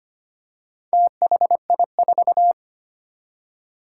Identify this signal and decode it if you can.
Morse code, heard as a keyed tone, about 25 wpm, 700 Hz. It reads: THI4
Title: Four Challenging Letter-Number Combinations 25wpm